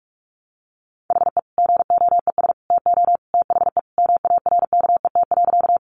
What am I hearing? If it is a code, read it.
Morse code, heard as a keyed tone, about 44 wpm, 700 Hz. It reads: HE GOES TO THE MARKET.
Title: Additional Easy Phrases 45wpm